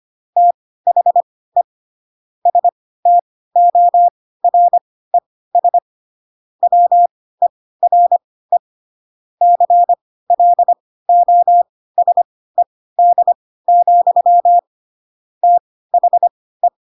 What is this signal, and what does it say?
Morse code, heard as a keyed tone, about 25 wpm, spaced as for 16 wpm, 700 Hz. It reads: THE STORES WERE CLOSED, THE